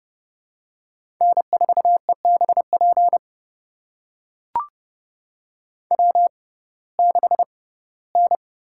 Morse code, heard as a keyed tone, about 31 wpm, spaced as for 19 wpm, 700 Hz. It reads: N4EBP W6N
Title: USA Call Signs - 6x Character Spacing - 30wpm